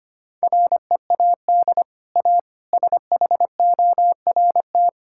Morse code, heard as a keyed tone, about 25 wpm, 690 Hz. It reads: READ A SHORT
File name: Easy Sentences with Hints - 25wpm